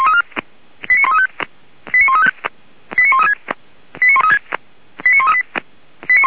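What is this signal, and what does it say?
Five-tone selective call (ZVEI), heard as quick runs of five tones, tones of about 70 ms. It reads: 89135 89135 89136
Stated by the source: Five-tone ZVEI selective call 89133, received by SDR, fire brigade Orlamünde (feuerwehr-orlamuende).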